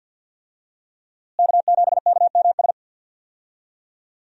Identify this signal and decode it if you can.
Morse code, heard as a keyed tone, about 52 wpm, 700 Hz. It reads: K7KMS
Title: USA Call Signs - 5 Characters - Repeated 3x - 50wpm